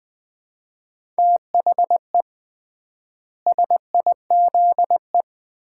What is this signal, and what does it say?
Morse code, heard as a keyed tone, about 20 wpm, 700 Hz. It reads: THE SIZE
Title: 3x Word-Spacing - Sets of 4 Words - 20wpm